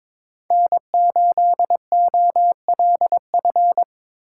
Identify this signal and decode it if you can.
Morse code, heard as a keyed tone, about 22 wpm, 700 Hz. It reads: N8OLF
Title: USA Call Signs - 8x Character Spacing - 22wpm